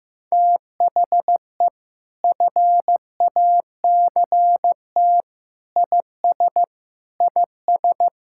Morse code, heard as a keyed tone, about 16 wpm, 700 Hz. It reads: THE FACT IS IS
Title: Sentences from Top 1600 Words 15wpm